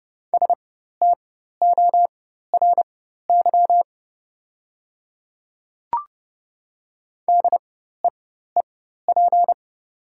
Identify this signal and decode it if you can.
Morse code, heard as a keyed tone, about 30 wpm, spaced as for 14 wpm, 700 Hz. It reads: STORY DEEP
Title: 4x Character Spacing - Top 500 Words - 30wpm